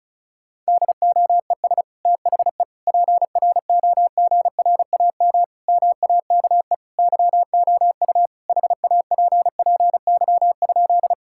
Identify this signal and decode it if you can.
Morse code, heard as a keyed tone, about 34 wpm, 700 Hz. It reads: DOES THE PROGRAM MAKE YOU HAPPY?
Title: Sentences from Top 1000 Words - Review Part 5 - 35wpm